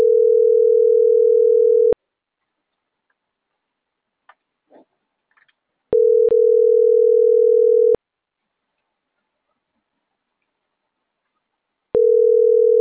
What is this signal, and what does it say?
Ringback tone (North America), 440 + 480 Hz.